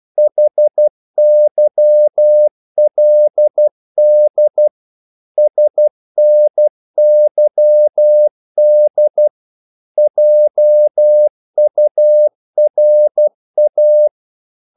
Morse code, heard as a keyed tone, about 12 wpm, 600 Hz. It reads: HYLD SNYD JURA
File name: Korte ord DK 12wpm